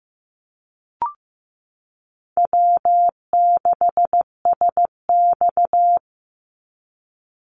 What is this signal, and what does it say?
Morse code, heard as a keyed tone, about 15 wpm, 700 Hz. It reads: W6SX
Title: Easy Call Signs - 15wpm